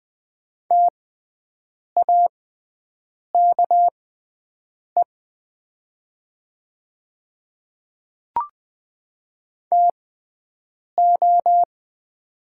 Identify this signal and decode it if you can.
Morse code, heard as a keyed tone, about 20 wpm, spaced as for 7 wpm, 700 Hz. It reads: TAKE TO
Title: 6x Character Spacing - Top 200 Words - 20wpm